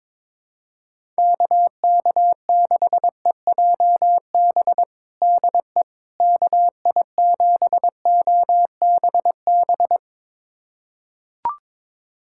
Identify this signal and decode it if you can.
Morse code, heard as a keyed tone, about 22 wpm, 700 Hz. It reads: KK6EJB DE KI7OBB
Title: USA Callsign DE Callsign - 6 Characters - 22wpm